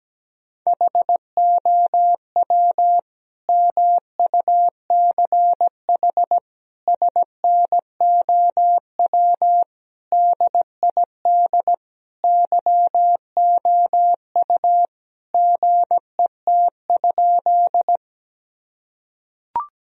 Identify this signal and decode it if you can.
Morse code, heard as a keyed tone, about 17 wpm, 700 Hz. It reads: HOW MUCH SNOW DID YOU GET?
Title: Sentences from Top 1000 Words - Review Part 3 - 17wpm